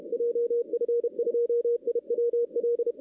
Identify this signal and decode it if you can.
Morse code, heard as a keyed tone, about 33 wpm, 470 Hz. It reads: JF2IWL
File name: 2016 arrl dx cw / 2016-ARRL-DX-CW-K5TR- - 0-26